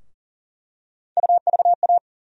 Morse code, heard as a keyed tone, about 40 wpm, 700 Hz. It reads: UVA
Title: USA Call Sign Postfixes - 3 Letters 40wpm